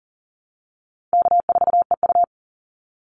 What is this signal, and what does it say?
Morse code, heard as a keyed tone, about 40 wpm, 700 Hz. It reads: K4EU
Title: Call Signs - 40wpm